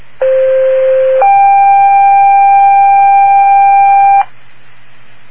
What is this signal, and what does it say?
Two-tone page, heard as a one-second tone, then a three-second tone, about 520 then 790 Hz. It is heard from 0.2 to 4.2 s.